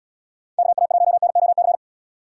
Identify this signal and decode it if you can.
Morse code, heard as a keyed tone, about 73 wpm, 700 Hz. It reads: KA1NOC